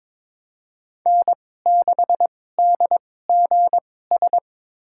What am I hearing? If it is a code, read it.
Morse code, heard as a keyed tone, about 22 wpm, spaced as for 16 wpm, 700 Hz. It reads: N6DGS
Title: USA Call Signs - 5 Characters - 2x Character Spacing - 22wpm